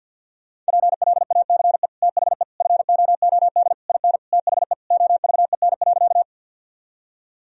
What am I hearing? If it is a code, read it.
Morse code, heard as a keyed tone, about 50 wpm, 700 Hz. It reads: PLACE THE FOOD IN THE OVEN.
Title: Sentences from Top 1000 Words - Review Part 4 - 50wpm